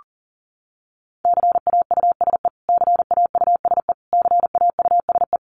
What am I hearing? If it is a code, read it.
Morse code, heard as a keyed tone, about 40 wpm, 700 Hz. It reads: CAUSE CAUSE CAUSE
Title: Top 200 Words Repeated 40wpm